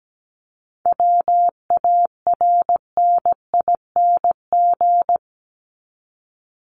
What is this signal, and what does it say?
Morse code, heard as a keyed tone, about 17 wpm, 700 Hz. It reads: WARNING
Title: Top 1751-2000 Words 17wpm